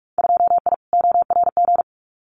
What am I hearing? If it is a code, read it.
Morse code, heard as a keyed tone, about 46 wpm, 700 Hz. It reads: 2I ORG
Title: Winter Field Day Exchanges Encore 45wpm